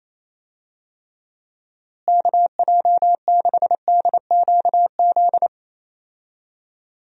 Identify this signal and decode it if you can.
Morse code, heard as a keyed tone, about 28 wpm, 700 Hz. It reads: KJ6DQZ
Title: USA Call Signs - 6 Characters - Repeated 2x - 28wpm